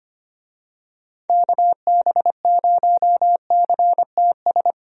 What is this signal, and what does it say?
Morse code, heard as a keyed tone, about 25 wpm, 700 Hz. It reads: KB0CTS
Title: USA Call Signs - 6 Characters - 25wpm